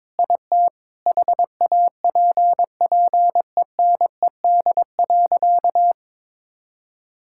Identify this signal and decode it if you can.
Morse code, heard as a keyed tone, about 22 wpm, 700 Hz. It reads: IT HAPPENED.
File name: practice / 22wpm / Sentences with 8-Word Phrases - 22wpm